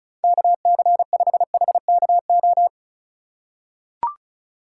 Morse code, heard as a keyed tone, about 34 wpm, 700 Hz. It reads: KC5HKO
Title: USA Call Signs - 6 Characters - 4x Character Spacing - 35wpm